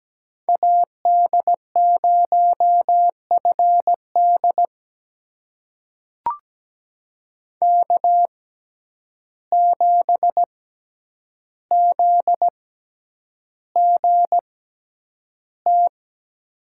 Morse code, heard as a keyed tone, about 17 wpm, spaced as for 9 wpm, 700 Hz. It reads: AD0FD K7ZGT
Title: USA Call Signs - 5 Characters - 6x Character Spacing - 17wpm